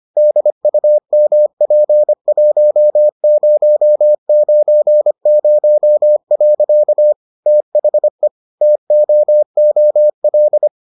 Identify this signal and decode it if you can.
Morse code, heard as a keyed tone, about 25 wpm, 600 Hz. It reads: DUMP1090. THE TOOL